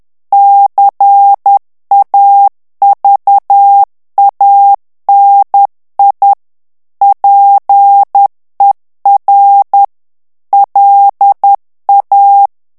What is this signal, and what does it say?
Morse code, heard as a keyed tone, about 10 wpm, 800 Hz. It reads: CAVANI PER LA
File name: morse mp3 / ITALIEN texte clair 1 vitesse 10 mots minute